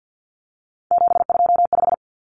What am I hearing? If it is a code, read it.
Morse code, heard as a keyed tone, about 50 wpm, 700 Hz. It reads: 725
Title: Common Cut Numbers - 3-Digit 50wpm